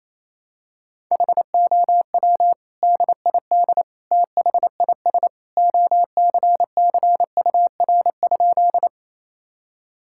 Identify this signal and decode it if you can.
Morse code, heard as a keyed tone, about 28 wpm, 700 Hz. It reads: HOW DID THIS OCCUR?
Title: Sentences from Top 1000 Words - Review Part 6 - 28wpm